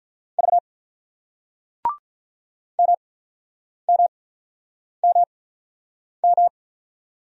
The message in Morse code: V MMMM

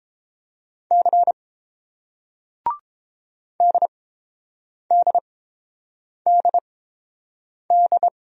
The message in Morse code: C DDDD

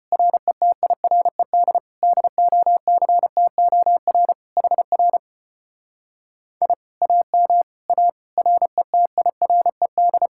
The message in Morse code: RETIRED DOCTOR HR I AM A RETIRED

